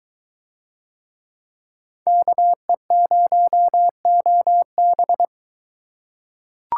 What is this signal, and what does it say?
Morse code, heard as a keyed tone, about 22 wpm, 700 Hz. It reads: KE0OB